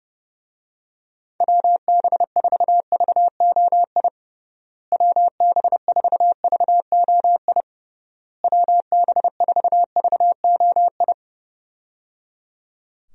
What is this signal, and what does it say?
Morse code, heard as a keyed tone, about 30 wpm, 700 Hz. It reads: WB4VOI WB4VOI WB4VOI
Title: USA Call Signs - 6 Characters - Repeated 3x - 30wpm